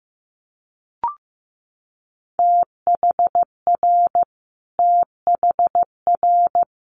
Morse code, heard as a keyed tone, about 15 wpm, 700 Hz. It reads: THR THR
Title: Multi-letter Phonemes Repeated 3x 15wpm